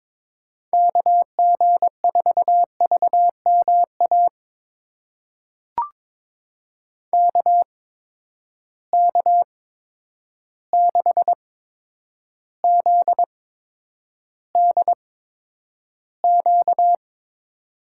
Morse code, heard as a keyed tone, about 22 wpm, spaced as for 9 wpm, 700 Hz. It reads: KG4VMA KK6ZDQ
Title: USA Call Signs - 6 Characters - 8x Character Spacing - 22wpm